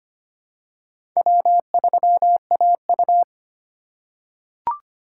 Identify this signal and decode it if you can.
Morse code, heard as a keyed tone, about 25 wpm, 700 Hz. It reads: W3AU
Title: USA Call Signs - 2x Character Spacing - 25wpm